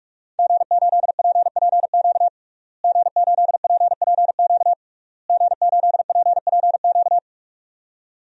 Morse code, heard as a keyed tone, about 41 wpm, 700 Hz. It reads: G8PPQ G8PPQ G8PPQ
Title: DXCC-100 Call Signs - 5 Characters - Repeated 3x - 45wpm